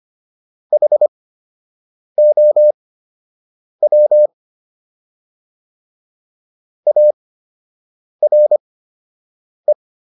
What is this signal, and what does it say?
Morse code, heard as a keyed tone, about 25 wpm, spaced as for 7 wpm, 600 Hz. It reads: HOW ARE